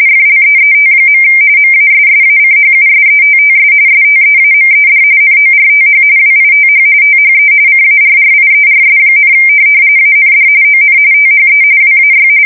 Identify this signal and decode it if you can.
RTTY, about 45.45 baud. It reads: FLEET. MASSACHUSETTS STEAMED NORTH WITH THE CARRIERS, AND ON THE WAY